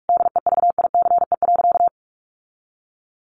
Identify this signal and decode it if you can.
Morse code, heard as a keyed tone, about 44 wpm, 700 Hz. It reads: DEVICE.
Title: Sentences from Top 1500 Words 45wpm